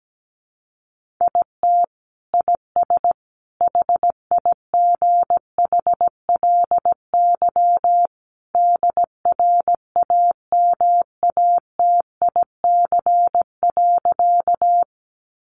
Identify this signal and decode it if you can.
Morse code, heard as a keyed tone, about 17 wpm, 700 Hz. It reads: IT IS HIGHLY DRAMATIC.
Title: Sentences from Top 1300 Words 17wpm